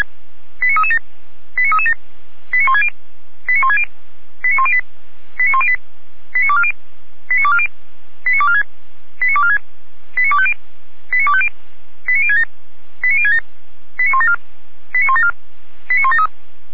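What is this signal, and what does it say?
Five-tone selective call (ZVEI), heard as quick runs of five tones, tones of about 80 ms. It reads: 89337 89337 89160 89160 89108 89108 89240 89240 89246 89246 89250 89250 89067 89067 89174 89174 89173
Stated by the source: ZVEI five-tone selective call received by SDR on channel 455.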